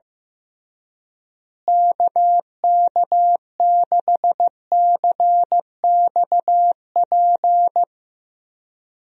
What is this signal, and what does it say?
Morse code, heard as a keyed tone, about 15 wpm, 700 Hz. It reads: KK6CXP
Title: USA Call Signs - 6 Characters - Repeated 2x - 15wpm